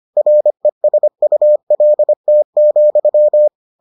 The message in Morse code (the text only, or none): RESULT,